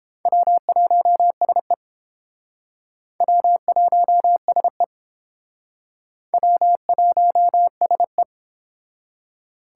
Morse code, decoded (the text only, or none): W1SE W1SE W1SE